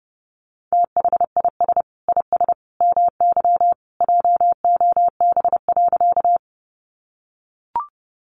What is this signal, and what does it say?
Morse code, heard as a keyed tone, about 30 wpm, 700 Hz. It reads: THIS IS MY JOB.
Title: Sentences from Top 600 Words 30wpm